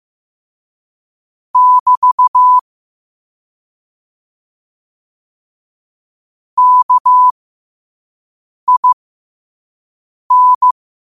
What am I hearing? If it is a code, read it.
Morse code, heard as a keyed tone, about 15 wpm, spaced as for 5 wpm, 1000 Hz. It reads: <BT> KIN
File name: qotdfiles / 5wpm